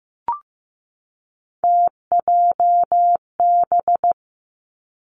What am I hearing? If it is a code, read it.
Morse code, heard as a keyed tone, about 15 wpm, 700 Hz. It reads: TJB
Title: Three Challenging Letter-Number Combinations 15wpm